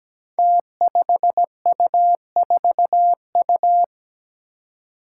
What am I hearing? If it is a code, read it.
Morse code, heard as a keyed tone, about 17 wpm, 700 Hz. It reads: T5U4U
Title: Five Challenging Letter-Number Combinations 17wpm